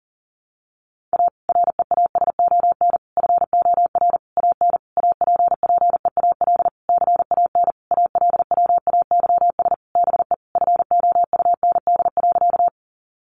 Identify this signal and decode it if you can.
Morse code, heard as a keyed tone, about 40 wpm, 700 Hz. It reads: A REASON FOR AN APPEAL CAN ALWAYS BE FOUND.
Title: Sentences from Top 1300 Words 40wpm